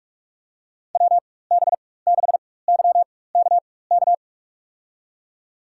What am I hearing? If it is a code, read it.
Morse code, heard as a keyed tone, about 42 wpm, spaced as for 20 wpm, 700 Hz. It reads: WB6YKK
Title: USA Call Signs - 6 Characters - 4x Character Spacing - 45wpm